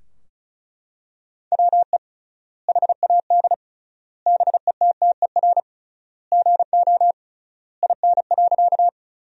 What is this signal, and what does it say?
Morse code, heard as a keyed tone, about 36 wpm, 700 Hz. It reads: WE HAD BETTER GO IN.
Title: 3x Word-Spacing - Sentences from Top 400 Words - 35wpm